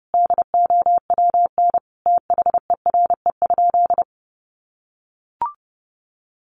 Morse code, heard as a keyed tone, about 28 wpm, 700 Hz. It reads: DOWN THERE?
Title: Sentences from Top 1100 Words 30wpm